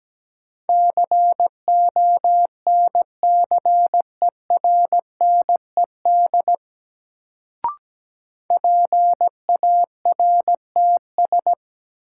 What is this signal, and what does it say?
Morse code, heard as a keyed tone, about 17 wpm, 700 Hz. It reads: CONCERNED PARTS
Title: Sentences from Top 1000 Words - Review Part 6 - New Word Drill - 17wpm